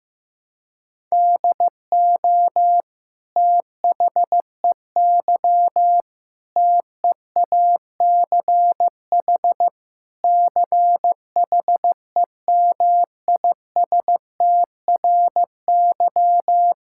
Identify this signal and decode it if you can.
Morse code, heard as a keyed tone, about 15 wpm, 700 Hz. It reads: DO THEY TEACH CHEMISTRY
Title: Sentences from Top 1000 Words - Review Part 6 - 15wpm